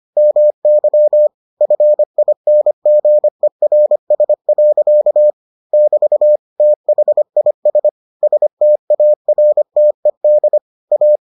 Morse code, heard as a keyed tone, about 25 wpm, 600 Hz. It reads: MY FINGERS. <BT> THIS STARTED A